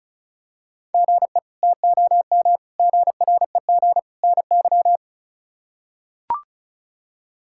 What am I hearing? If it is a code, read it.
Morse code, heard as a keyed tone, about 34 wpm, 700 Hz. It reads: GE TOM GREG NY
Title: K1USN SST QSO Practice - 35wpm